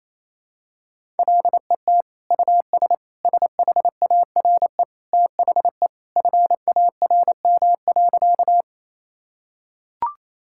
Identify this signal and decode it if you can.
Morse code, heard as a keyed tone, about 28 wpm, 700 Hz. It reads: LET US SHARE THE FARM.
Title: Sentences from Top 1000 Words - Review Part 6 - 28wpm